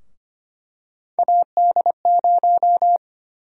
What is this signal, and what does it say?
Morse code, heard as a keyed tone, about 25 wpm, 700 Hz. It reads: AD0
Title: USA Call Sign Prefixes - 25wpm